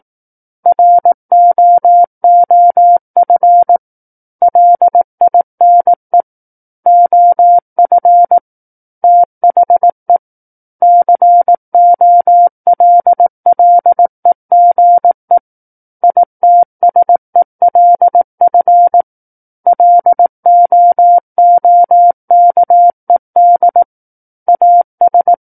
Morse code, heard as a keyed tone, about 18 wpm, 700 Hz. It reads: ROOF LINE OF THE COLLEGE ITSELF LOOKED AS
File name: Book One / 09-Chapter 9 - 18 WPM